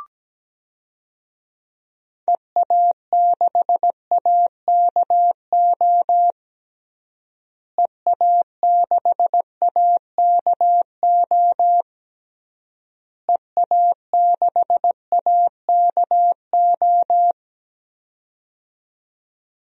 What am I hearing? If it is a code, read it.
Morse code, heard as a keyed tone, about 17 wpm, 700 Hz. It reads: EA6AKO EA6AKO EA6AKO
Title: DXCC-100 Call Signs - 6 Characters - Repeated 3x - 17wpm